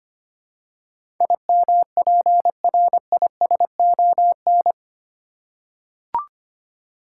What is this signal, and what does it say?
Morse code, heard as a keyed tone, about 25 wpm, 700 Hz. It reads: IMPRISON
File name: Eight Letter Words - 25wpm